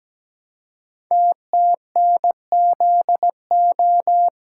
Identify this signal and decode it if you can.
Morse code, heard as a keyed tone, about 17 wpm, 700 Hz. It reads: TTNZO
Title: Code Groups - 5 Letters - 17wpm